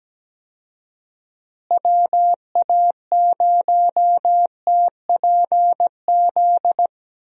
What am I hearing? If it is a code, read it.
Morse code, heard as a keyed tone, about 17 wpm, 700 Hz. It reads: WA0TPZ
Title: USA Call Signs - 6 Characters - Repeated 2x - 17wpm